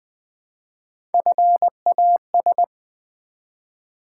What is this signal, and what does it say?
Morse code, heard as a keyed tone, about 20 wpm, 700 Hz. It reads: FAS
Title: Build A Word - 1 Letter At A Time - 4-Letter Words - 20wpm